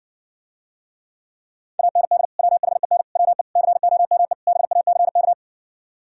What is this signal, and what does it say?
Morse code, heard as a keyed tone, about 60 wpm, 700 Hz. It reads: AND WHEN WE COME BACK